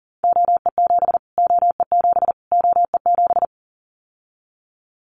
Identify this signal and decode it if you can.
Morse code, heard as a keyed tone, about 40 wpm, 700 Hz. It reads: OE7 OE7 OE7
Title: Callsign Prefixes for Top 25 DXCC - Repeated 3x - 40wpm